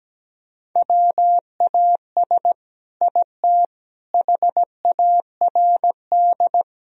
Morse code, heard as a keyed tone, about 17 wpm, 700 Hz. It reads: WAS IT HARD
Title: Sentences from Top 1000 Words - Review Part 4 - 17wpm